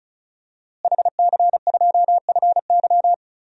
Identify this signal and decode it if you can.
Morse code, heard as a keyed tone, about 36 wpm, 700 Hz. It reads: HC2FY
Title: DXCC-100 Call Signs - 5 Characters - Repeated 2x - 35wpm